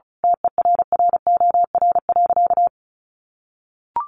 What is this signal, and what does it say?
Morse code, heard as a keyed tone, about 34 wpm, 700 Hz. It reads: TERROR.
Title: A Sound Of Thunder 35wpm